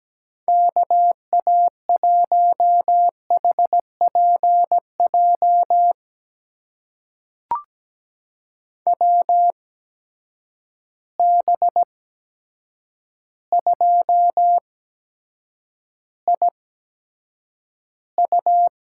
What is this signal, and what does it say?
Morse code, heard as a keyed tone, about 17 wpm, spaced as for 8 wpm, 700 Hz. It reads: KA1HPJ WB2IU
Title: USA Call Signs - 6 Characters - 8x Character Spacing - 17wpm